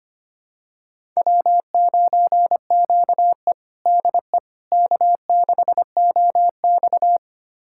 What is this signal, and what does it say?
Morse code, heard as a keyed tone, about 25 wpm, 700 Hz. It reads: W9QE DE K6OX
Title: USA Callsign DE Callsign - 4 Characters - 25wpm